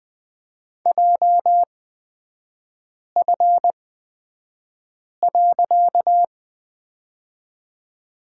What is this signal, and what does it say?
Morse code, heard as a keyed tone, about 20 wpm, spaced as for 5 wpm, 700 Hz. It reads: JF.
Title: 255 Lesson 41e - Period - Three Characters 20wpm